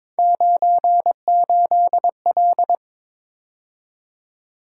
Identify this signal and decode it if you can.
Morse code, heard as a keyed tone, about 26 wpm, 700 Hz. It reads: 98L
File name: Speed-Racing - Three Letter-Number Combinations - 22wpm